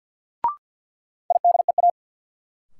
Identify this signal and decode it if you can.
Morse code, heard as a keyed tone, about 49 wpm, 700 Hz. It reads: IDEA
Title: Sets of 3 Words - New Word Drill - 50wpm